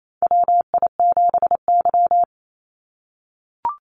Morse code, heard as a keyed tone, about 28 wpm, 700 Hz. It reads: WI7Y
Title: USA Call Signs 28wpm